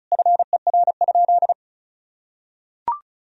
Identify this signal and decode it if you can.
Morse code, heard as a keyed tone, about 36 wpm, 700 Hz. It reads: FER?
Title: Sentences from Top 200 Words and Abbreviations 35wpm